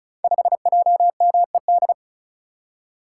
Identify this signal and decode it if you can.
Morse code, heard as a keyed tone, about 34 wpm, 700 Hz. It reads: 5JMED
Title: Code Groups - 5 Letter:Numbers - 35wpm